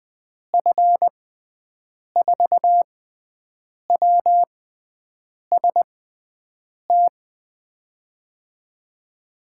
Morse code, heard as a keyed tone, about 20 wpm, spaced as for 7 wpm, 700 Hz. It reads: F4WST
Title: USA Call Signs - 6 Characters - 6x Character Spacing - 20wpm